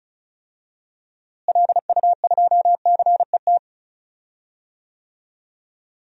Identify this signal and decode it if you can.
Morse code, heard as a keyed tone, about 36 wpm, 700 Hz. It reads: LU2CET
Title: DXCC-100 Call Signs - 6 Characters - Repeated 2x - 35wpm